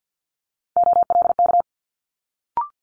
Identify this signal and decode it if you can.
Morse code, heard as a keyed tone, about 50 wpm, 700 Hz. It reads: OLK
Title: Multi-letter Phonemes Quick Fire 50wpm